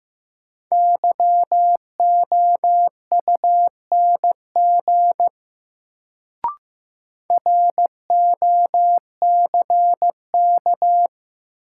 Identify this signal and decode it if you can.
Morse code, heard as a keyed tone, about 15 wpm, 700 Hz. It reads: YOUNG ROCK
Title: Sentences from Top 400 Words - New Word Drill - 15wpm